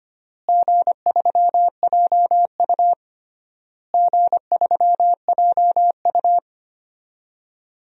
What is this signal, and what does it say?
Morse code, heard as a keyed tone, about 25 wpm, 700 Hz. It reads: G3JU G3JU
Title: DXCC-100 Call Signs - 4 Characters - Repeated 3x - 25wpm